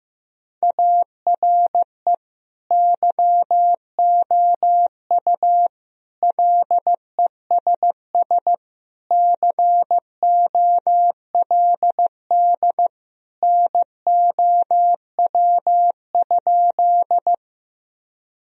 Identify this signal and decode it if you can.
Morse code, heard as a keyed tone, about 15 wpm, 700 Hz. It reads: ARE YOU LESS COLD NOW?
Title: Sentences from Top 1000 Words - Review Part 2 - 15wpm